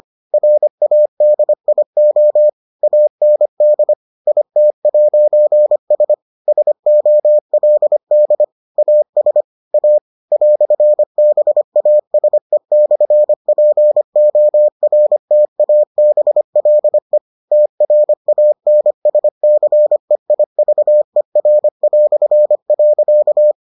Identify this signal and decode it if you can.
Morse code, heard as a keyed tone, about 25 wpm, 600 Hz. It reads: RADIO AND IT'S SOLD AS A "BASE/PORTABLE TRANSCEIVER".